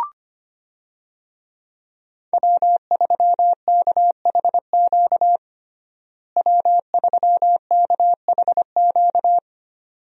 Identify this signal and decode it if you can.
Morse code, heard as a keyed tone, about 25 wpm, 700 Hz. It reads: W3KHQ W3KHQ
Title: USA Call Signs - 5 Characters - Repeated 2x - 25wpm